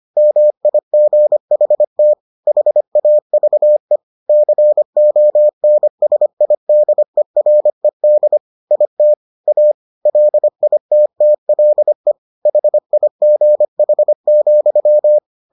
Morse code, heard as a keyed tone, about 25 wpm, 600 Hz. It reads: MIGHT HAVE CONSIDERED IT A LITTLE HIGH,